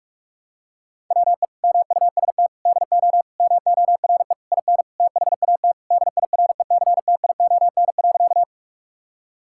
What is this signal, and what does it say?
Morse code, heard as a keyed tone, about 42 wpm, 700 Hz. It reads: WE MUST DO MORE IN THAT DIRECTION.
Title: Sentences from Top 1000 Words - Review Part 5 - 45wpm